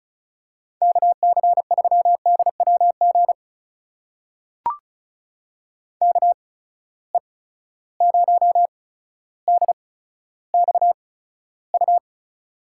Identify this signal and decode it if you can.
Morse code, heard as a keyed tone, about 36 wpm, spaced as for 15 wpm, 700 Hz. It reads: KC3DWG KE0DXU
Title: USA Call Signs - 6 Characters - 8x Character Spacing - 35wpm